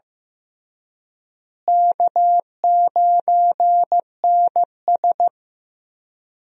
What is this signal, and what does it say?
Morse code, heard as a keyed tone, about 15 wpm, 700 Hz. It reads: K9NS
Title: USA Call Signs - 4 Characters - Repeated 3x - 15wpm